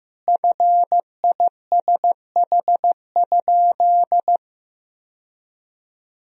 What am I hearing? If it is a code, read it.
Morse code, heard as a keyed tone, about 16 wpm, 700 Hz. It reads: FISH?
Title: Sentences from Top 1000 Words - Review Part 1 - 15wpm